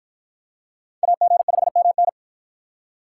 Morse code, heard as a keyed tone, about 48 wpm, 700 Hz. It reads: AG5MN